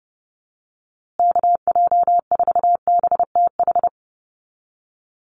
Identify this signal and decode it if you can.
Morse code, heard as a keyed tone, about 30 wpm, 700 Hz. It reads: KJ4BTH
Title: USA Call Signs Repeated 3x 30wpm